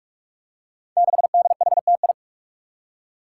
Morse code, heard as a keyed tone, about 44 wpm, 700 Hz. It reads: 6DHTI
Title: Five Challenging Letter-Number Combinations 45wpm